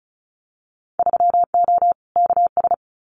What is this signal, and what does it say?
Morse code, heard as a keyed tone, about 36 wpm, 700 Hz. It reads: 3O KS
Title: Winter Field Day Exchanges Encore 35wpm